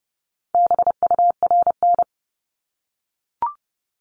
Morse code, heard as a keyed tone, about 30 wpm, 700 Hz. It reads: BURN
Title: Multi-letter Phonemes Repeated 3x 30wpm